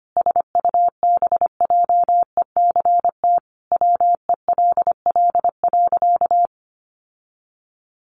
Morse code, heard as a keyed tone, about 25 wpm, 700 Hz. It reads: SUBJECT WELL.
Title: Sentences from Top 800 Words 25wpm